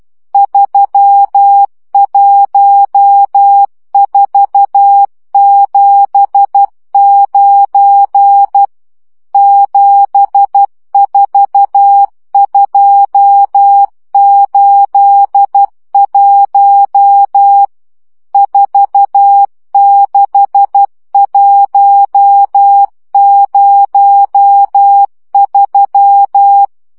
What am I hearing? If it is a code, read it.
Morse code, heard as a keyed tone, about 12 wpm, 800 Hz. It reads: 31479 74281 46103
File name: morze vizsga 2011-1 10